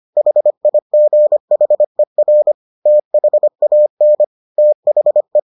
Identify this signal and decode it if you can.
Morse code, heard as a keyed tone, about 25 wpm, 600 Hz. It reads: HIGHER THAN THE